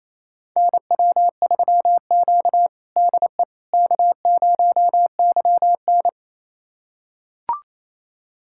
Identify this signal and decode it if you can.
Morse code, heard as a keyed tone, about 28 wpm, 700 Hz. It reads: NW3Q DE K0YN
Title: USA Callsign DE Callsign - 4 Characters - 28wpm